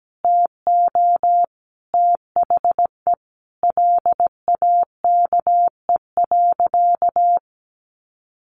Morse code, heard as a keyed tone, about 17 wpm, 700 Hz. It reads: TO THE LAKE.
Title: Sentences from Top 1500 Words 17wpm